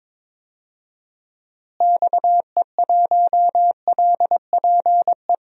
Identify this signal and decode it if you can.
Morse code, heard as a keyed tone, about 22 wpm, 700 Hz. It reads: XE1LPE
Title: DXCC-100 Call Signs - 6 Characters - Repeated 3x - 22wpm